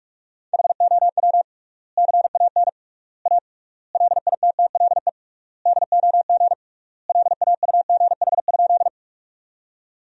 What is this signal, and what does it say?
Morse code, heard as a keyed tone, about 46 wpm, 700 Hz. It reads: HOW CAN A LITTLE DOG LAUGH?
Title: 3x Word-Spacing - Sentences from Top 500 Words - 45wpm